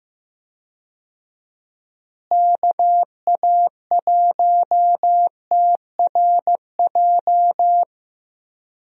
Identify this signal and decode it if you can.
Morse code, heard as a keyed tone, about 16 wpm, 700 Hz. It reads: KA1TRJ
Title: USA Call Signs - 6 Characters - Repeated 2x - 15wpm